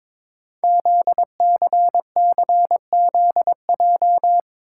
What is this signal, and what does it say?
Morse code, heard as a keyed tone, about 22 wpm, 700 Hz. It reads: ZCCZJ
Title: Code Groups - 5 Letters - 22wpm